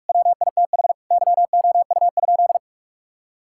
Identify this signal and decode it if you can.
Morse code, heard as a keyed tone, about 42 wpm, 700 Hz. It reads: WITH YOU?
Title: Sentences from Top 1000 Words - Review Part 6 - 45wpm